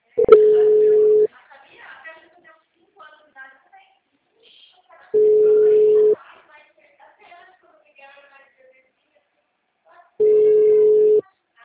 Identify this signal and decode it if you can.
Ringback tone (Europe, 425 Hz), with speech.